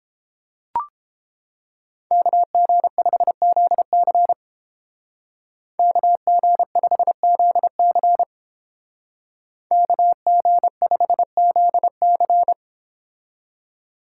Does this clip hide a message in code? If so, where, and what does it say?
Yes: Morse code from 2.1 to 12.5 s, heard as a keyed tone, about 30 wpm, 700 Hz. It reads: KG5ZC KG5ZC KG5ZC